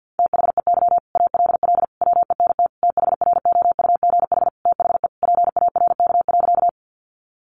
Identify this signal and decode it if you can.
Morse code, heard as a keyed tone, about 50 wpm, 700 Hz. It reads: THEY ALL WENT THROUGH THE PARK.
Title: Sentences from Top 1300 Words 50wpm